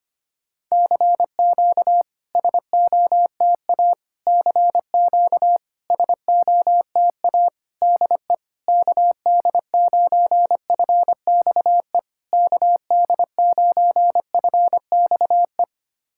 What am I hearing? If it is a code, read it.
Morse code, heard as a keyed tone, about 25 wpm, 700 Hz. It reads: CQ SOTA CQ SOTA DE KD9FXE KD9FXE
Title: SOTA QSO Practice 25wpm